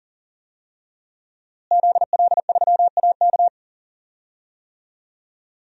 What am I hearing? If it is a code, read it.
Morse code, heard as a keyed tone, about 40 wpm, 700 Hz. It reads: ZL3AK
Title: DXCC-100 Call Signs - 5 Characters - Repeated 2x - 40wpm